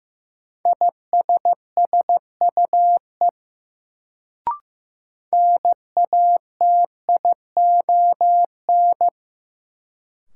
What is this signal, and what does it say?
Morse code, heard as a keyed tone, about 15 wpm, 700 Hz. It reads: ISSUE NATION
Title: Sentences from Top 900 Words - New Word Drill - 15wpm